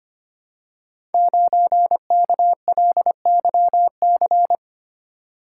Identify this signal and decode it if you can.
Morse code, heard as a keyed tone, about 25 wpm, 700 Hz. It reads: 9KLYC
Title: Code Groups - 5 Letter:Numbers - 25wpm